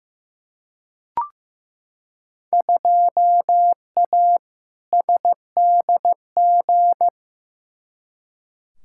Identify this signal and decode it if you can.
Morse code, heard as a keyed tone, about 15 wpm, 700 Hz. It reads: 2A SDG
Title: Field Day Exchanges Encore - 15wpm